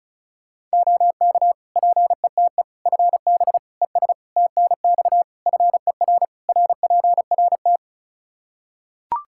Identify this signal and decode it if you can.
Morse code, heard as a keyed tone, about 36 wpm, 700 Hz. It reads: OK PETE FB ES TNX FER RPRT
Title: Practice QSO - Exchange 2 - Info - 35wpm